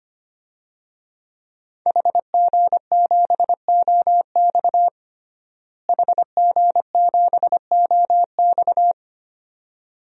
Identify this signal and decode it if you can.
Morse code, heard as a keyed tone, about 25 wpm, 700 Hz. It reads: HG7OX HG7OX
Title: DXCC-100 Call Signs - 5 Characters - Repeated 2x - 25wpm